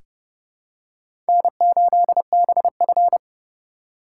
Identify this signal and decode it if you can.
Morse code, heard as a keyed tone, about 30 wpm, 700 Hz. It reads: N8BF
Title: USA Call Signs - 4 Characters - 3x Character Spacing - 30wpm